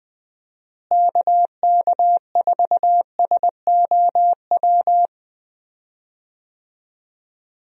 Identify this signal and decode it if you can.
Morse code, heard as a keyed tone, about 20 wpm, 700 Hz. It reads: KK4SOW
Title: USA Call Signs - 6 Characters - Repeated 2x - 20wpm